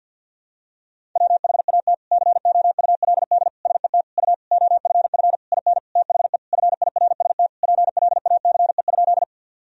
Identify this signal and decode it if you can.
Morse code, heard as a keyed tone, about 52 wpm, 700 Hz. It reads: WHAT COULD SET U OFF IN THE FIRST PLACE?
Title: Sentences from Top 200 Words and Abbreviations 50wpm